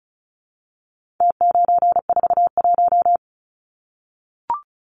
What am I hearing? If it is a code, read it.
Morse code, heard as a keyed tone, about 34 wpm, 700 Hz. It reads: T941
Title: Numbers - Contest Style - 4-Digit 35wpm